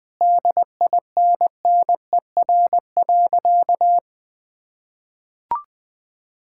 Morse code, heard as a keyed tone, about 20 wpm, 700 Hz. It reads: DINNER.
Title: Difficult Sentences with Hints - 20wpm